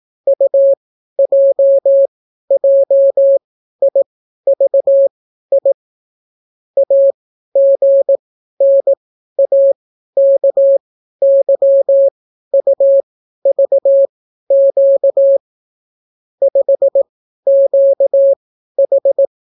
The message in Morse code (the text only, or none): UJJIVI AGNAKYUVQ 5QH